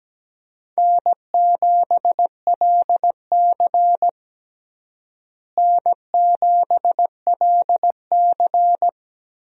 Morse code, heard as a keyed tone, about 17 wpm, 700 Hz. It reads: N7LC N7LC
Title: USA Call Signs - 4 Characters - Repeated 2x - 17wpm